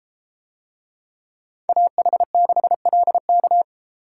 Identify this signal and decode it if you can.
Morse code, heard as a keyed tone, about 32 wpm, 700 Hz. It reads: AH6LK